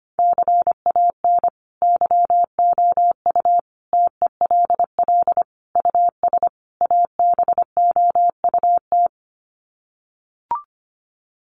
Morse code, heard as a keyed tone, about 25 wpm, 700 Hz. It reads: CAN YOU TELL US ABOUT
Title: Sets of 5 Words Encore2 25wpm